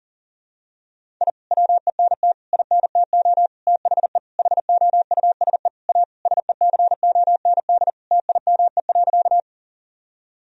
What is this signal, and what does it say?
Morse code, heard as a keyed tone, about 40 wpm, 700 Hz. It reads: I WENT INTO THE HOUSE A SECOND TIME.
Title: Sentences from Top 1000 Words - Review Part 1 - 40wpm